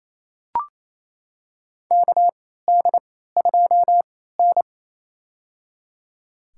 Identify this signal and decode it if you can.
Morse code, heard as a keyed tone, about 28 wpm, spaced as for 16 wpm, 700 Hz. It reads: KD2N
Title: USA Call Signs - 4 Characters - 3x Character Spacing - 28wpm